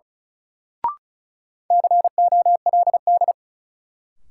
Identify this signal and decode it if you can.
Morse code, heard as a keyed tone, about 36 wpm, 700 Hz. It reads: COLD
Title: Sets of 5 Words - New Word Drill - 35wpm